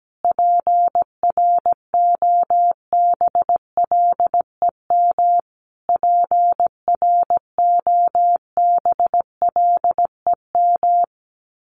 Morse code, heard as a keyed tone, about 17 wpm, 700 Hz. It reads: PROBLEM PROBLEM
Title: Top 500 Words Repeated 17wpm